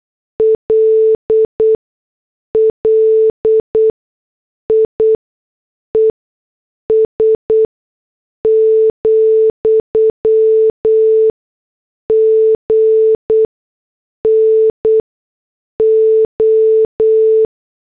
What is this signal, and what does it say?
Morse code, heard as a keyed tone, about 8 wpm, spaced as for 6 wpm, 440 Hz. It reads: LLIES, GNO